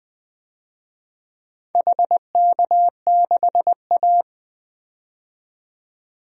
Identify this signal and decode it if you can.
Morse code, heard as a keyed tone, about 20 wpm, 700 Hz. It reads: HK6A
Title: DXCC-100 Call Signs - 4 Characters - 20wpm